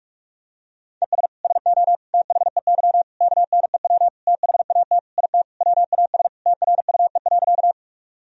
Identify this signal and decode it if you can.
Morse code, heard as a keyed tone, about 46 wpm, 700 Hz. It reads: ES SO THEY KNEW THAT IT WAS TRUE.
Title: Sentences from Top 400 Words and Abbreviations 45wpm